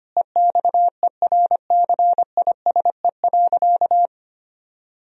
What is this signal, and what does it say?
Morse code, heard as a keyed tone, about 25 wpm, 700 Hz. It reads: EXERCISE.
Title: Sentences from Top 1000 Words - Review Part 4 - 25wpm